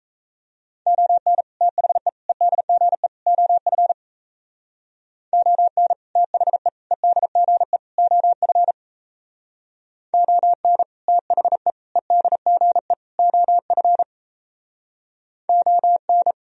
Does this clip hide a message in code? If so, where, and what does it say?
Yes: Morse code from 0.9 to 16.3 s, heard as a keyed tone, about 37 wpm, 700 Hz. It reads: ON THE EDGE OF ON THE EDGE OF ON THE EDGE OF ON